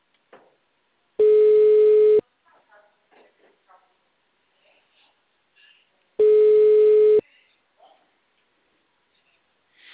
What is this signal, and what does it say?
Ringback tone (Europe, 425 Hz).